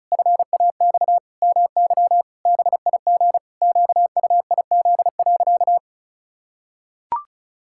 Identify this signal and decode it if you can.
Morse code, heard as a keyed tone, about 36 wpm, 700 Hz. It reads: FAX MY BIG QUIZ.
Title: Pangram Challenge v2 - 35wpm